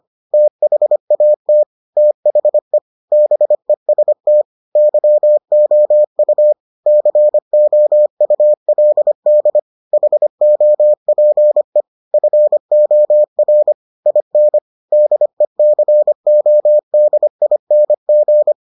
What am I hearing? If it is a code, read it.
Morse code, heard as a keyed tone, about 25 wpm, 600 Hz. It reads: THAT THE BEST YOU COULD HOPE FOR IN DECODING